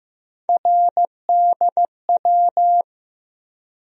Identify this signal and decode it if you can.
Morse code, heard as a keyed tone, about 15 wpm, 700 Hz. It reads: RDW
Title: USA Call Sign Postfixes - 3 Letters 15wpm